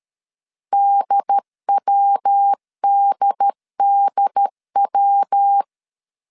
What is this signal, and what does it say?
Morse code, heard as a keyed tone, about 13 wpm, 800 Hz. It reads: DWDDW
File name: CW / DET / 1 3 1500C S C 12wpm 2750